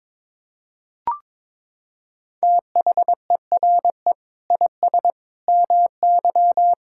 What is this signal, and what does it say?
Morse code, heard as a keyed tone, about 22 wpm, 700 Hz. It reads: THERE IS MY
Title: Sentences from Top 1000 Words - Review Part 6 - 22wpm